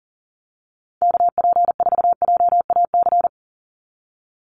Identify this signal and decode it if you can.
Morse code, heard as a keyed tone, about 40 wpm, 690 Hz. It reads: KP4JAC
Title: USA Call Signs 40wpm